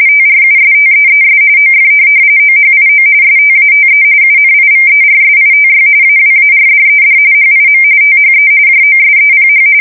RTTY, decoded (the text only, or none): TRUSTEES OF OHIO STATE UNIVERSITY AWARDED HIM THE JOSEPH